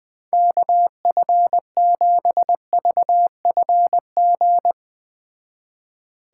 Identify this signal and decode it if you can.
Morse code, heard as a keyed tone, about 21 wpm, 700 Hz. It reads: KF7VFG